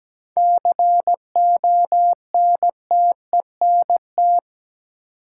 Morse code, heard as a keyed tone, about 17 wpm, 700 Hz. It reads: CONTENT
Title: Sentences from Top 900 Words - New Word Drill - 17wpm